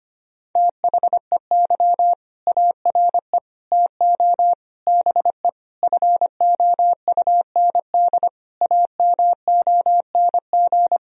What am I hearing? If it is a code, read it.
Morse code, heard as a keyed tone, about 25 wpm, 700 Hz. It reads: THEY ARE TO BE FOUND AMONG
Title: Sentences from Top 1000 Words - Review Part 3 - 25wpm